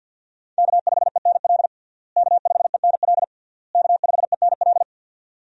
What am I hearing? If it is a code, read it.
Morse code, heard as a keyed tone, about 49 wpm, 700 Hz. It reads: K5ENL K5ENL K5ENL
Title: USA Call Signs - 5 Characters - Repeated 3x - 50wpm